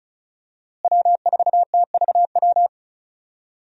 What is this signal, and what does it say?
Morse code, heard as a keyed tone, about 36 wpm, 700 Hz. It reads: W4TVW
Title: USA Call Signs - 5 Characters - Repeated 3x - 35wpm